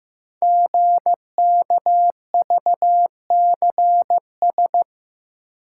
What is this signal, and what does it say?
Morse code, heard as a keyed tone, about 15 wpm, 700 Hz. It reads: GKVCS
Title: Code Groups - 5 Letters - 15wpm